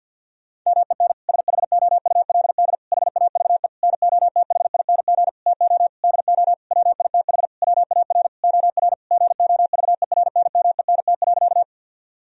Morse code, heard as a keyed tone, about 52 wpm, 700 Hz. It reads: MEN SHOULD HAVE NOTHING TO DO WITH WAR OR GOVERNMENT.
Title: Sentences from Top 1000 Words - Review Part 3 - 50wpm